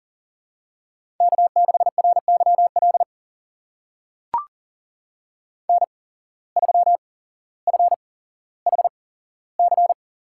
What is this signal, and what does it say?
Morse code, heard as a keyed tone, about 40 wpm, spaced as for 17 wpm, 700 Hz. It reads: K6RYL N3FHC
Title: USA Call Signs - 5 Characters - 8x Character Spacing - 40wpm